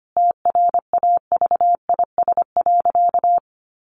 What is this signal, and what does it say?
Morse code, heard as a keyed tone, about 25 wpm, 700 Hz. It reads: TRAVIS.
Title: A Sound Of Thunder 25wpm